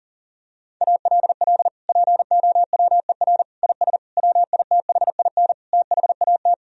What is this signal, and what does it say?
Morse code, heard as a keyed tone, about 40 wpm, 700 Hz. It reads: ALL POWER IS WITHIN THAT
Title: Sentences from Top 1000 Words - Review Part 4 - 40wpm